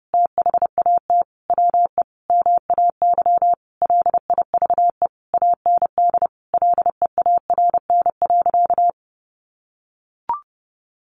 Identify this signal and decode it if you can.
Morse code, heard as a keyed tone, about 30 wpm, 700 Hz. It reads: THAT WE MAY LIVE AND LEARN.
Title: Sentences from Top 800 Words 30wpm